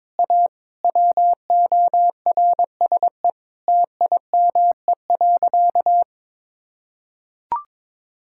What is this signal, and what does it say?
Morse code, heard as a keyed tone, about 21 wpm, 700 Hz. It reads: A WORSE TIME.
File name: practice / 22wpm / Sentences with 8-Word Phrases - 22wpm